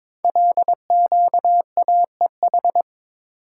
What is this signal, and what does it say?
Morse code, heard as a keyed tone, about 23 wpm, 690 Hz. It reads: LQAEH
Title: Code Groups - 5 Letters - 22wpm